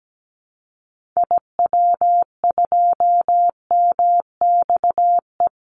Morse code, heard as a keyed tone, about 17 wpm, 700 Hz. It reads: IW2MXE
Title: Call Signs Repeated 2x - 17wpm